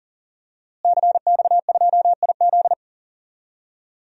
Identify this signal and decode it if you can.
Morse code, heard as a keyed tone, about 40 wpm, 700 Hz. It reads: CX2IZ
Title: DXCC-100 Call Signs - 5 Characters - Repeated 2x - 40wpm